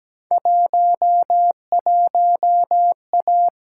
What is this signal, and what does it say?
Morse code, heard as a keyed tone, about 17 wpm, 700 Hz. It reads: 11A
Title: Field Day Exchanges Encore - 17wpm